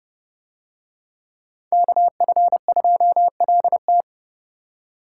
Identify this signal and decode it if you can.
Morse code, heard as a keyed tone, about 29 wpm, 700 Hz. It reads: KF2LT